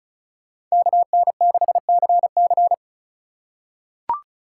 Morse code, heard as a keyed tone, about 36 wpm, 700 Hz. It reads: KN6CC
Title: USA Call Signs - 5 Characters - 4x Character Spacing - 35wpm